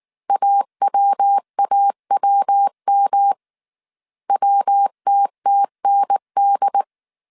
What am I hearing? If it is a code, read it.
Morse code, heard as a keyed tone, about 19 wpm, 800 Hz. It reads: AWAWM WTTND